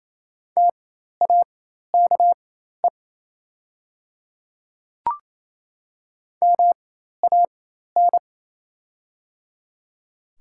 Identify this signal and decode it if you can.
Morse code, heard as a keyed tone, about 28 wpm, spaced as for 13 wpm, 700 Hz. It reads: TAKE MAN
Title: 4x Character Spacing - Top 500 Words - 28wpm